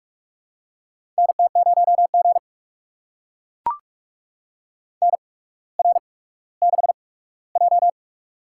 Morse code, heard as a keyed tone, about 44 wpm, spaced as for 19 wpm, 700 Hz. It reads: NT0G NR6J
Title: USA Call Signs - 4 Characters - 8x Character Spacing - 45wpm